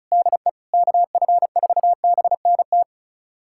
Morse code, heard as a keyed tone, about 36 wpm, 690 Hz. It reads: DE KF4BNT
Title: USA Callsign DE Callsign - 6 Characters - 35wpm